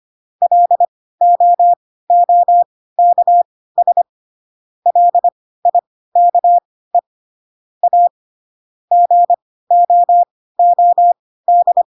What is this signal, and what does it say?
Morse code, heard as a keyed tone, about 25 wpm, spaced as for 16 wpm, 700 Hz. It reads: LOOKS LIKE A GOOD
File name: Pinocchio / Pinocchio ch3 10 25 ik1hge